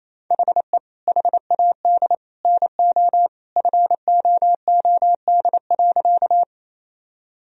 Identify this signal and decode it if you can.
Morse code, heard as a keyed tone, about 28 wpm, 700 Hz. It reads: HE HAD NO FOOD.
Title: Sentences from Top 1000 Words - Review Part 1 - 28wpm